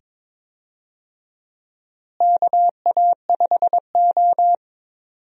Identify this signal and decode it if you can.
Morse code, heard as a keyed tone, about 22 wpm, 700 Hz. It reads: KA5O
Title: USA Call Signs - 4 Characters - Repeated 2x - 22wpm